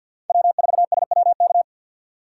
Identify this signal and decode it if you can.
Morse code, heard as a keyed tone, about 53 wpm, 690 Hz. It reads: W4SWK
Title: USA Call Signs - 2x Character Spacing - 50wpm